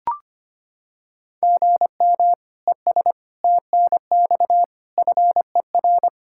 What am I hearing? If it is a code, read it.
Morse code, heard as a keyed tone, about 25 wpm, 700 Hz. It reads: GM ES TNX FER
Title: Practice QSO - Exchange 1 - Beginning - 25wpm